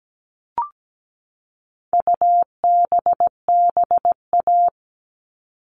Morse code, heard as a keyed tone, about 17 wpm, 700 Hz. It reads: UBBA
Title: All Cut Numbers - 4-Digit 17wpm